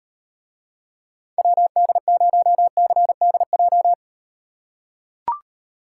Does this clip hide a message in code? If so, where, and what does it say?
Yes: Morse code from 1.4 to 3.9 s, heard as a keyed tone, about 38 wpm, 700 Hz. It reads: WD0CDJ